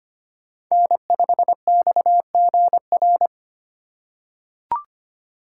Morse code, heard as a keyed tone, about 25 wpm, 700 Hz. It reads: N5XGR
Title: USA Call Signs - 5 Characters - 2x Character Spacing - 25wpm